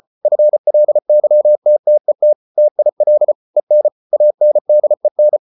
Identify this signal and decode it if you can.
Morse code, heard as a keyed tone, about 34 wpm, 600 Hz. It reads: FLYTTET TIL EN ANDEN
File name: Ferie QRQ 34wpm